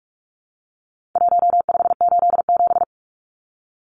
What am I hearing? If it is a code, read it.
Morse code, heard as a keyed tone, about 44 wpm, 700 Hz. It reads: 1587
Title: Numbers - Contest Style - 4-Digit 45wpm